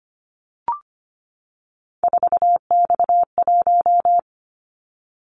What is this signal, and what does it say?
Morse code, heard as a keyed tone, about 25 wpm, 700 Hz. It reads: 4X1
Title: Callsign Prefixes for Top 50 DXCC - 25wpm